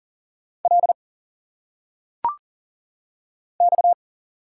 Morse code, heard as a keyed tone, about 38 wpm, 700 Hz. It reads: L X
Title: Single Letters - Phonetic Pronunciation 40wpm